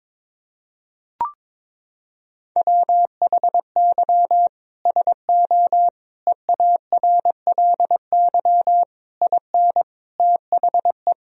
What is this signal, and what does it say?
Morse code, heard as a keyed tone, about 22 wpm, 700 Hz. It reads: WHY SO EARLY IN THE
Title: Sentences from Top 1000 Words - Review Part 2 - 22wpm